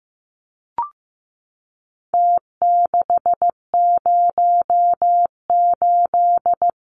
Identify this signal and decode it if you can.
Morse code, heard as a keyed tone, about 15 wpm, 700 Hz. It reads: T608
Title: Numbers - Contest Style - 4-Digit 15wpm